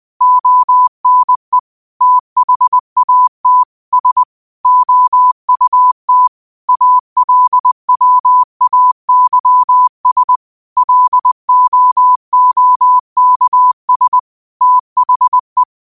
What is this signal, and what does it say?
Morse code, heard as a keyed tone, about 20 wpm, 1000 Hz. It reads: ONE THAT S OUT ALWAYS LOOKS THE
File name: qotdfiles / 20wpm